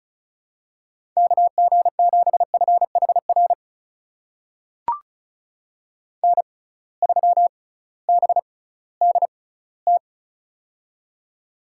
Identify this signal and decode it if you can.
Morse code, heard as a keyed tone, about 36 wpm, spaced as for 19 wpm, 700 Hz. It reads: KG7FHR N3BDT
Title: USA Call Signs - 6x Character Spacing - 35wpm